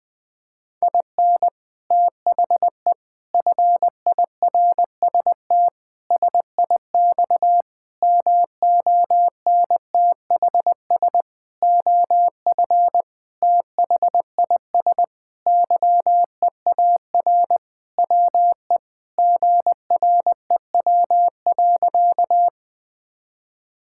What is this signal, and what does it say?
Morse code, heard as a keyed tone, about 21 wpm, 700 Hz. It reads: IN THE FIRST SIX MONTHS OF THIS YEAR WE GREW.
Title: Sentences with 8-Word Phrases - 20wpm